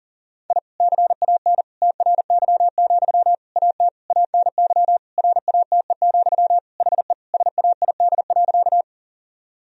Morse code, heard as a keyed tone, about 40 wpm, 700 Hz. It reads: I CAN TRY, AT ANY RATE, HE SAID.
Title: Sentences from Top 1000 Words - Review Part 4 - 40wpm